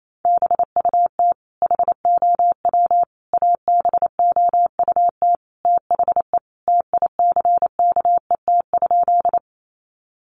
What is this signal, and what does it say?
Morse code, heard as a keyed tone, about 28 wpm, 700 Hz. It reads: BUT HOW ABOUT THE TICKET?
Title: Sentences from Top 1400 Words 28wpm